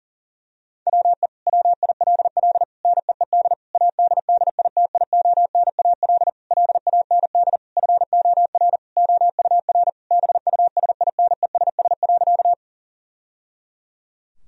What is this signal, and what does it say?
Morse code, heard as a keyed tone, about 40 wpm, 700 Hz. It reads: WE WILL NEED ADDITIONAL LAND FOR OUR BUSINESS.
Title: Sentences from Top 1000 Words - Review Part 4 - 40wpm